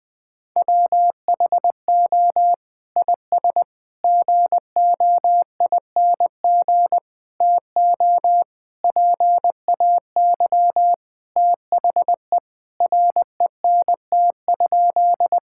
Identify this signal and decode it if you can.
Morse code, heard as a keyed tone, about 20 wpm, 700 Hz. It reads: WHO IS GOING TO PAY THE RENT?
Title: Sentences from Top 1000 Words - Review Part 5 - 20wpm